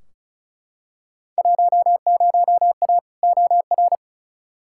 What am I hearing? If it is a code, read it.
Morse code, heard as a keyed tone, about 36 wpm, 700 Hz. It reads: 10A OR
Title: Field Day Exchanges - 35wpm